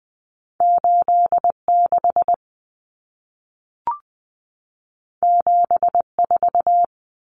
Morse code, heard as a keyed tone, about 20 wpm, 700 Hz. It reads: 86 74
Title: Common Cut Numbers - 2-Digit 20wpm